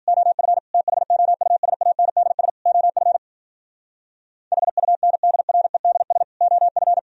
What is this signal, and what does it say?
Morse code, heard as a keyed tone, about 56 wpm, 700 Hz. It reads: OF THOUSANDS OF HUNDREDS OF